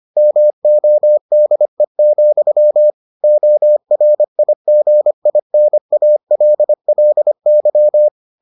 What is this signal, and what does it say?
Morse code, heard as a keyed tone, about 25 wpm, 600 Hz. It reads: MODE, ORIGINALLY